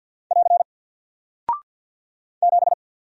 Morse code, heard as a keyed tone, about 48 wpm, 700 Hz. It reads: <AR> 7
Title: Single Character (Letter:Number:Punctuation:Prosign) Alt 50wpm